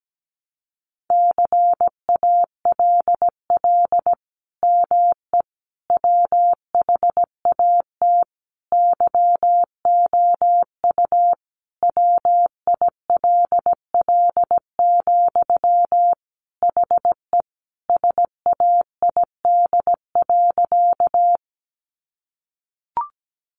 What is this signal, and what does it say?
Morse code, heard as a keyed tone, about 17 wpm, 700 Hz. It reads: CALL ME WHAT YOU WILL, HE SAID.
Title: Sentences from Top 200 Words 17wpm